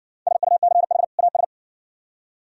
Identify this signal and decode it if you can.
Morse code, heard as a keyed tone, about 67 wpm, 700 Hz. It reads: SUCH AS